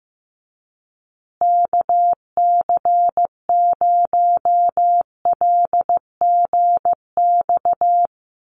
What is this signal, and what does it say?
Morse code, heard as a keyed tone, about 15 wpm, 700 Hz. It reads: KC0LGX
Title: USA Call Signs Repeated 2x 15wpm